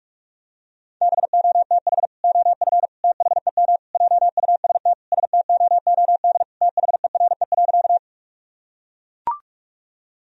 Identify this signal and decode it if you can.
Morse code, heard as a keyed tone, about 42 wpm, 700 Hz. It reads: BOTH OF THEM JUST STOOD THERE.
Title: Sentences from Top 1000 Words - Review Part 2 - 45wpm